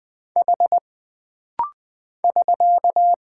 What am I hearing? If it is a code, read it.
Morse code, heard as a keyed tone, about 20 wpm, 700 Hz. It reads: H <SK>
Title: Single Character (Letter:Number:Punctuation:Prosign) - Rapid-Fire - 20wpm